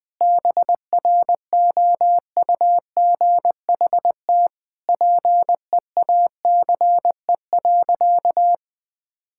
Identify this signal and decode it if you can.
Morse code, heard as a keyed tone, about 20 wpm, 700 Hz. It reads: BROUGHT PEACE.
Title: Sentences with 5-Word Phrases - 20wpm